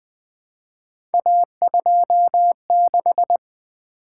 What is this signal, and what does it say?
Morse code, heard as a keyed tone, about 20 wpm, 700 Hz. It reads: A26
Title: Three Letter-Number Combinations - 20wpm